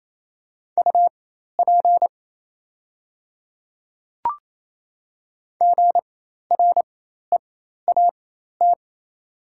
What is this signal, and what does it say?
Morse code, heard as a keyed tone, about 27 wpm, spaced as for 13 wpm, 690 Hz. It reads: UP GREAT
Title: 4x Character Spacing - Top 200 Words - 28wpm